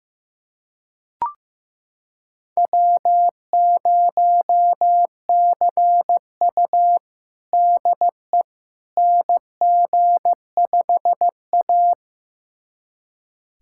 Morse code, heard as a keyed tone, about 16 wpm, 700 Hz. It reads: W0CU DE NG5A
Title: USA Callsign DE Callsign - 4 Characters - 15wpm